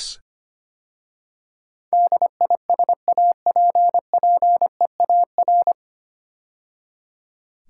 Morse code, heard as a keyed tone, about 25 wpm, 700 Hz. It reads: DISAPPEAR
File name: practice / 25wpm / 9 Letter Words - First 3 Letters Given - 25wpm